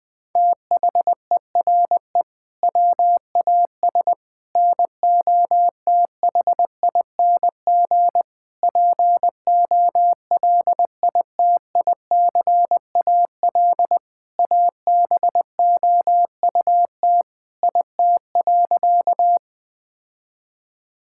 Morse code, heard as a keyed tone, about 19 wpm, 700 Hz. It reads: THERE WAS NOTHING POLITICAL ABOUT IT.
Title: Sentences from Top 1000 Words - Review Part 5 - 20wpm